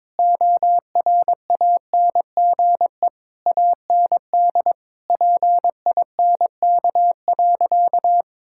Morse code, heard as a keyed tone, about 22 wpm, 700 Hz. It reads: ORANGE AND PINK.
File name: Difficult Sentences with Hints - 22wpm